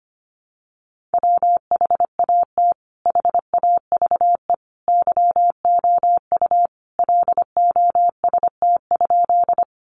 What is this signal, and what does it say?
Morse code, heard as a keyed tone, about 25 wpm, 700 Hz. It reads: WHAT HAVE YOU LOST?
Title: Sentences from Top 1400 Words 25wpm